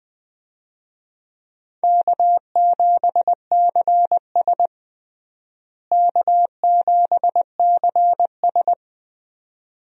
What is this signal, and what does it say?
Morse code, heard as a keyed tone, about 21 wpm, 700 Hz. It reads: K7CS K7CS
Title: USA Call Signs - 4 Characters - Repeated 2x - 20wpm